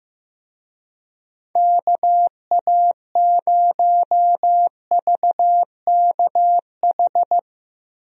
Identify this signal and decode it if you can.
Morse code, heard as a keyed tone, about 15 wpm, 700 Hz. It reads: KA0VKH
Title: USA Call Signs - 6 Characters - Repeated 3x - 15wpm